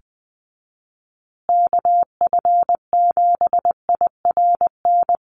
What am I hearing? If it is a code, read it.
Morse code, heard as a keyed tone, about 20 wpm, 700 Hz. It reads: KF7IRN
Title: USA Call Signs Repeated 2x 20wpm